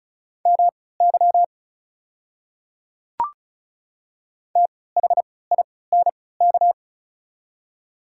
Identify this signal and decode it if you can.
Morse code, heard as a keyed tone, about 34 wpm, spaced as for 20 wpm, 700 Hz. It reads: MY THINK
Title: 3x Character Spacing - Top 300 Words - 35wpm